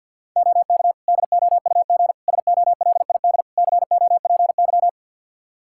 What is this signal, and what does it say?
Morse code, heard as a keyed tone, about 52 wpm, 700 Hz. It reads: OK DOUG SOLID COPY
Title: K1USN SST QSO Practice 50wpm